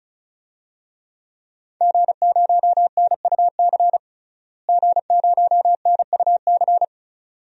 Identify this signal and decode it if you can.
Morse code, heard as a keyed tone, about 36 wpm, 700 Hz. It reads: G0NUC G0NUC
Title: DXCC-100 Call Signs - 5 Characters - Repeated 2x - 35wpm